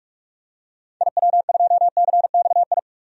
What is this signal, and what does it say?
Morse code, heard as a keyed tone, about 46 wpm, 700 Hz. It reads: IW2CXI
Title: DXCC-100 Call Signs - 6 Characters - 45wpm